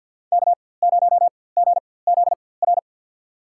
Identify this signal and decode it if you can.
Morse code, heard as a keyed tone, about 53 wpm, spaced as for 24 wpm, 700 Hz. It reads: K0GZR
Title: USA Call Signs - 4x Character Spacing - 50wpm